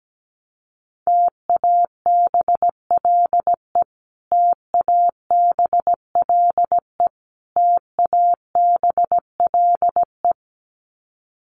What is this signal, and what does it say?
Morse code, heard as a keyed tone, about 17 wpm, 700 Hz. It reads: TABLE TABLE TABLE
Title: Top 500 Words Repeated 17wpm